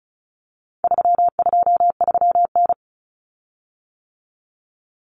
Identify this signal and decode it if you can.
Morse code, heard as a keyed tone, about 34 wpm, 700 Hz. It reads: 323N
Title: Common Cut Numbers - 4-Digit 35wpm